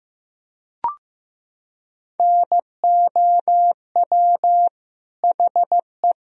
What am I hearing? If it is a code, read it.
Morse code, heard as a keyed tone, about 15 wpm, 700 Hz. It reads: NOW HE
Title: Sentences from Top 1000 Words - Review Part 4 - 15wpm